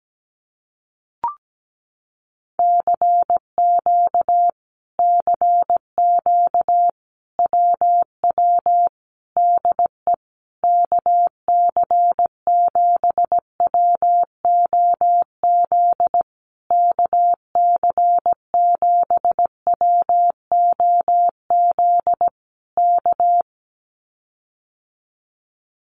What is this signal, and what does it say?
Morse code, heard as a keyed tone, about 17 wpm, 700 Hz. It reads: CQ CQ WW DE KC7WOZ KC7WOZ K
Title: Common QSO Phrases Encore 17wpm